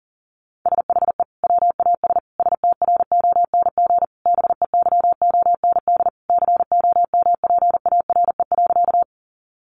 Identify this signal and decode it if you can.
Morse code, heard as a keyed tone, about 40 wpm, 700 Hz. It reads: SHE WAS STRONG BEYOND COMPARE.
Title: Sentences from Top 1300 Words 40wpm